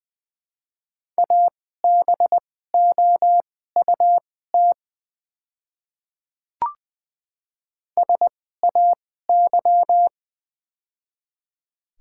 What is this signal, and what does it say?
Morse code, heard as a keyed tone, about 20 wpm, spaced as for 14 wpm, 700 Hz. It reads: ABOUT SAY
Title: 2x Character Spacing - Top 300 Words - 20wpm